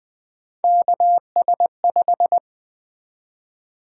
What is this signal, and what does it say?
Morse code, heard as a keyed tone, about 20 wpm, 700 Hz. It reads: KS5
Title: Three Letter-Number Combinations - 20wpm